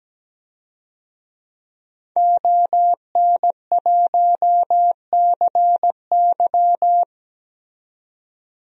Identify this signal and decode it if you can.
Morse code, heard as a keyed tone, about 17 wpm, 700 Hz. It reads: ON1CY
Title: DXCC-100 Call Signs - 5 Characters - 17wpm